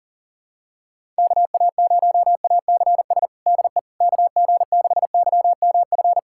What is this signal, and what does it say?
Morse code, heard as a keyed tone, about 40 wpm, 700 Hz. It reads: KA0ACS DE KG6YMF
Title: USA Callsign DE Callsign 40wpm